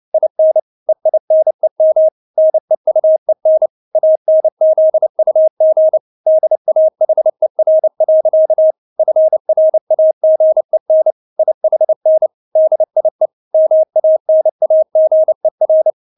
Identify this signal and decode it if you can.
Morse code, heard as a keyed tone, about 29 wpm, 630 Hz. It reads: IN EINEM NEUEN ANZUG DAHER. FRAGEN IHN DIE MANAGER
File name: Witz10 29wpm